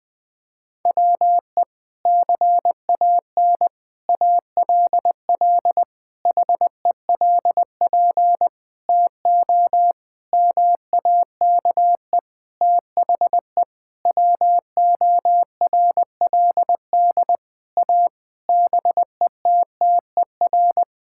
Morse code, heard as a keyed tone, about 20 wpm, 700 Hz. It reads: WE CAN ALL HELP TO MAKE THE WORLD A BETTER
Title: Sentences with 6-Word Phrases - 20wpm